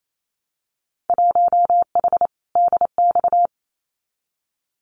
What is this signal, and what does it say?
Morse code, heard as a keyed tone, about 28 wpm, 700 Hz. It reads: 1H DX
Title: Winter Field Day Exchanges 28wpm